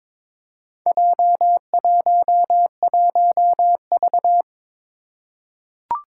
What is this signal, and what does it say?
Morse code, heard as a keyed tone, about 22 wpm, 700 Hz. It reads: J11V
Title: Four Challenging Letter-Number Combinations 22wpm